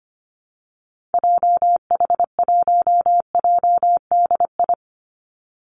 Morse code, heard as a keyed tone, about 25 wpm, 700 Hz. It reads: JH1JDI
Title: Call Signs - 25wpm